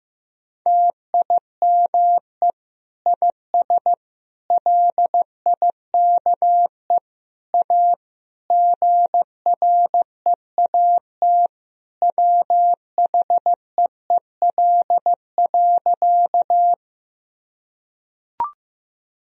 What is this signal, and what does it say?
Morse code, heard as a keyed tone, about 15 wpm, 700 Hz. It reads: TIME IS LIKE A GREAT WHEEL.
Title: Sentences from Top 1000 Words - Review Part 3 - 15wpm